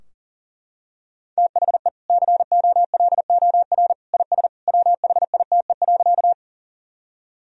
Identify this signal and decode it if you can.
Morse code, heard as a keyed tone, about 38 wpm, 700 Hz. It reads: THE COLOR IS WHITE.
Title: Sentences from Top 1000 Words - Review Part 2 - 40wpm